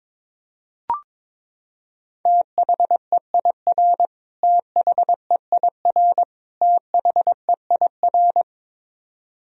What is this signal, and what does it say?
Morse code, heard as a keyed tone, about 22 wpm, 700 Hz. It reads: THEIR THEIR THEIR
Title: Top 100 Words Repeated 22wpm